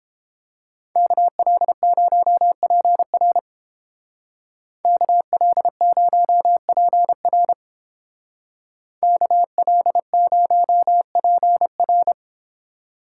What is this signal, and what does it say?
Morse code, heard as a keyed tone, about 28 wpm, 700 Hz. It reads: KL0PR KL0PR KL0PR